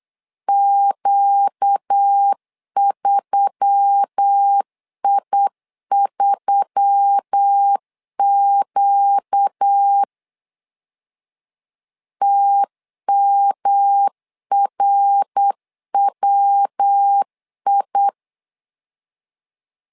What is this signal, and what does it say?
Morse code, heard as a keyed tone, about 8 wpm, 800 Hz. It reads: Q3I3Q TMRWI
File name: LEZIONI / 4 Q1S3 8wpm 3037